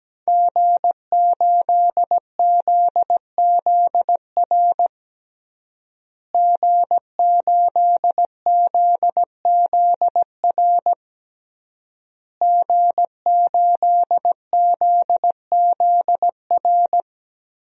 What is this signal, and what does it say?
Morse code, heard as a keyed tone, about 17 wpm, 700 Hz. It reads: G8ZZR G8ZZR G8ZZR
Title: DXCC-100 Call Signs - 5 Characters - Repeated 3x - 17wpm